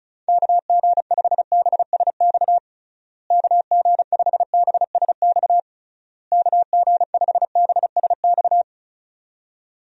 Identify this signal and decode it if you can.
Morse code, heard as a keyed tone, about 36 wpm, 700 Hz. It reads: KG5BSX KG5BSX KG5BSX
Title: USA Call Signs - 6 Characters - Repeated 3x - 35wpm